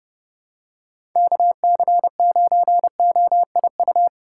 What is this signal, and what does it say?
Morse code, heard as a keyed tone, about 30 wpm, 700 Hz. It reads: KC9OIU
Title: USA Call Signs - 6 Characters - 2x Character Spacing - 30wpm